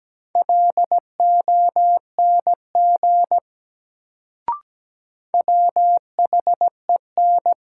Morse code, heard as a keyed tone, about 17 wpm, 700 Hz. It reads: LONG WHEN
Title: Sentences from Top 100 Words - New Word Drill - 17wpm